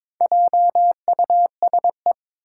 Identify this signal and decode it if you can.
Morse code, heard as a keyed tone, about 22 wpm, 700 Hz. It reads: JUSE
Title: Four Challenging Letter-Number Combinations 22wpm